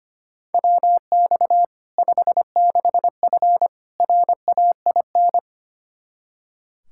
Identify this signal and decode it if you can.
Morse code, heard as a keyed tone, about 25 wpm, 700 Hz. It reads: WX 56F RAIN
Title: Practice QSO - Exchange 2 - Info - 25wpm